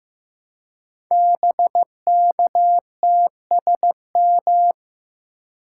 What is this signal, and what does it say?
Morse code, heard as a keyed tone, about 15 wpm, 700 Hz. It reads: BKTSM
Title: Code Groups - 5 Letters - 15wpm